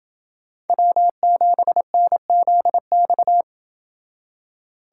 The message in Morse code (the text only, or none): W7NZX